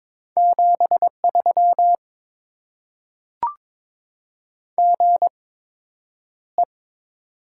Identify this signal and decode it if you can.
Morse code, heard as a keyed tone, about 22 wpm, spaced as for 9 wpm, 700 Hz. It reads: 73 GE
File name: Top 20 QSO Elements - 8x Character Spacing - 22wpm